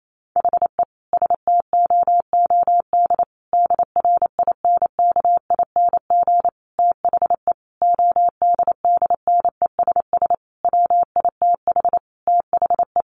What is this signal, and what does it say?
Morse code, heard as a keyed tone, about 28 wpm, 700 Hz. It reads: HE STOOD DRINKING THE ODDNESS WITH THE
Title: A Sound Of Thunder 28wpm